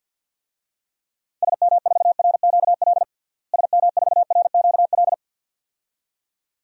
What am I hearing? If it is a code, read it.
Morse code, heard as a keyed tone, about 48 wpm, 700 Hz. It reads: SM4RQL SM4RQL
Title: DXCC-100 Call Signs - 6 Characters - Repeated 2x - 50wpm